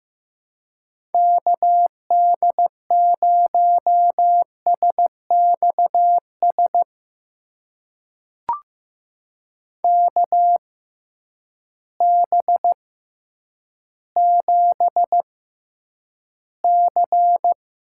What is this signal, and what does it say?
Morse code, heard as a keyed tone, about 15 wpm, spaced as for 9 wpm, 700 Hz. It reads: KD0SXS KB7C
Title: USA Call Signs - 6 Characters - 6x Character Spacing - 15wpm